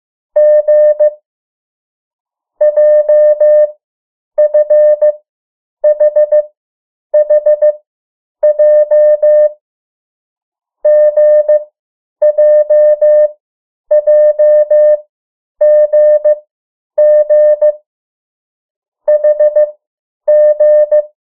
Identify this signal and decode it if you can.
Morse code, heard as a keyed tone, about 15 wpm, 600 Hz. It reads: G J F H H J G J J G G H G
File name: morse mp3 / 03 urok 2